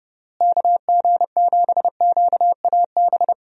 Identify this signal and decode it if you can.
Morse code, heard as a keyed tone, about 30 wpm, 700 Hz. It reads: KG7QAB
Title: USA Call Signs - 6 Characters - Repeated 3x - 30wpm